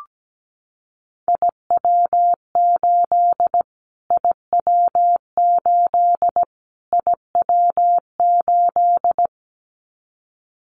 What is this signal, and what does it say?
Morse code, heard as a keyed tone, about 17 wpm, 700 Hz. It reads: IW8 IW8 IW8
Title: Callsign Prefixes for Top 25 DXCC - Repeated 3x - 17wpm